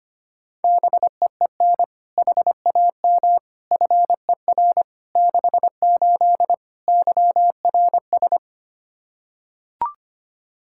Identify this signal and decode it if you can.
Morse code, heard as a keyed tone, about 25 wpm, 700 Hz. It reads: BEEN HAM FER 68 YRS
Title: K1USN SST QSO Practice 25wpm